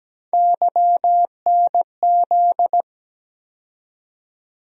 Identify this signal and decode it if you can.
Morse code, heard as a keyed tone, about 17 wpm, 700 Hz. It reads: YNZ
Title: USA Call Sign Postfixes - 3 Letters 17wpm